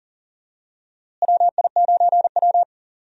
Morse code, heard as a keyed tone, about 40 wpm, 700 Hz. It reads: WI9W
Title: USA Call Signs - 4 Characters - Repeated 3x - 40wpm